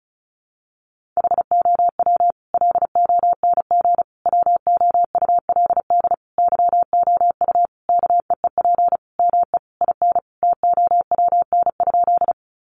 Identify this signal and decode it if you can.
Morse code, heard as a keyed tone, about 34 wpm, 700 Hz. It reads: HOW LONG WOULD YOU KEEP ME IN TOWN?
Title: Sentences from Top 500 Words 35wpm